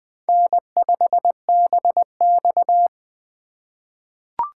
Morse code, heard as a keyed tone, about 21 wpm, 700 Hz. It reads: N5BX
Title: USA Call Signs - 4 Characters - 4x Character Spacing - 20wpm